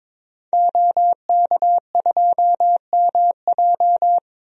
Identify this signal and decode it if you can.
Morse code, heard as a keyed tone, about 22 wpm, 700 Hz. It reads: OK2MJ
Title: DXCC-100 Call Signs - 5 Characters - 22wpm